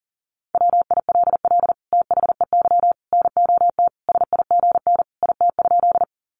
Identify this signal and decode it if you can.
Morse code, heard as a keyed tone, about 40 wpm, 700 Hz. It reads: WILL THEY NOT SIGN IT?
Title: Sentences from Top 1000 Words 40wpm